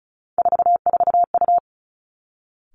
Morse code, heard as a keyed tone, about 34 wpm, 700 Hz. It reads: 44U
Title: Three Challenging Letter-Number Combinations 35wpm